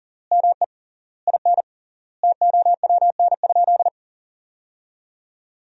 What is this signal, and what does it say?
Morse code, heard as a keyed tone, about 40 wpm, 700 Hz. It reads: ME IN TOWN?
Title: 3x Word-Spacing - Sentences from Top 500 Words - 40wpm